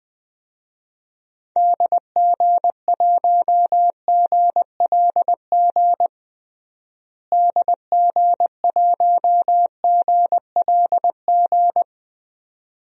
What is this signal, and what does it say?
Morse code, heard as a keyed tone, about 20 wpm, 700 Hz. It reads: DG1GLG DG1GLG
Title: DXCC-100 Call Signs - 6 Characters - Repeated 2x - 20wpm